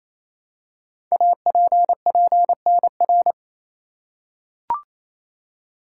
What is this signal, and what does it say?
Morse code, heard as a keyed tone, about 28 wpm, 700 Hz. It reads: APPNR
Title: Code Groups - 5 Letter:Numbers - 28wpm